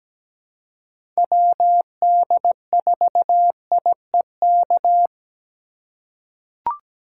Morse code, heard as a keyed tone, about 17 wpm, 700 Hz. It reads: WD4IEK
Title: USA Call Signs - 6x Character Spacing - 17wpm